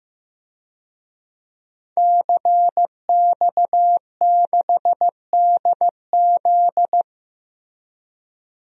Morse code, heard as a keyed tone, about 15 wpm, 700 Hz. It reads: CX6DZ
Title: DXCC-100 Call Signs - 5 Characters - Repeated 3x - 15wpm